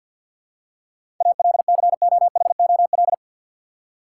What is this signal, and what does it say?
Morse code, heard as a keyed tone, about 53 wpm, 700 Hz. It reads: ALCOHOL
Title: Seven Letter Words - 50wpm